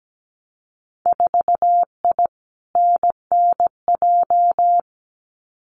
Morse code, heard as a keyed tone, about 17 wpm, 700 Hz. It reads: 4I NNJ
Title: Winter Field Day Exchanges 17wpm